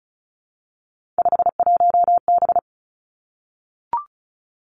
Morse code, heard as a keyed tone, about 36 wpm, 700 Hz. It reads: 51B
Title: Three Challenging Letter-Number Combinations 35wpm